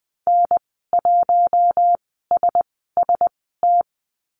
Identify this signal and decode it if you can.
Morse code, heard as a keyed tone, about 20 wpm, spaced as for 15 wpm, 700 Hz. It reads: N1SST
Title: 062 Lesson 11g - Letter h - Call Signs 20wpm